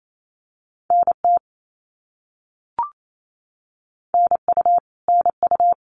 Morse code, heard as a keyed tone, about 28 wpm, 700 Hz. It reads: NT NU NU
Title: Canadian Provinces and Territories - Repeated 3x - 28wpm